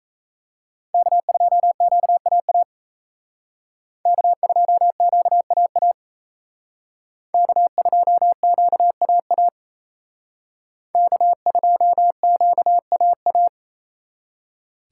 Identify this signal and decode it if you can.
Morse code, heard as a keyed tone, about 35 wpm, 700 Hz. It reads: K2QAA K2QAA K2QAA K2QAA